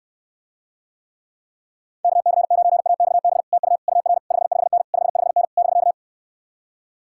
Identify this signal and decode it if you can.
Morse code, heard as a keyed tone, about 75 wpm, 700 Hz. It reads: KC9AXB TU UR 55N 55N <BK>